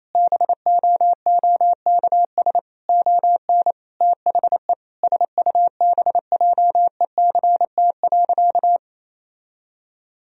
Morse code, heard as a keyed tone, about 27 wpm, 700 Hz. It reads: BOOKS ON THE SUBJECT.
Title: Sentences from Top 1000 Words - Review Part 4 - 28wpm